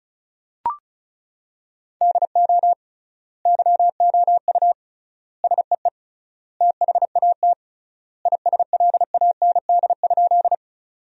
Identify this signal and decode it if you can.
Morse code, heard as a keyed tone, about 36 wpm, 700 Hz. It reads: DO YOU SEE THAT ISLAND?
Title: 3x Word-Spacing - Sentences from Top 500 Words - 35wpm